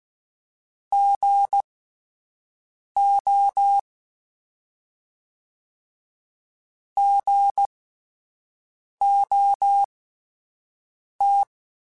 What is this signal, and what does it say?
Morse code, heard as a keyed tone, about 16 wpm, spaced as for 5 wpm, 800 Hz. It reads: GO GOT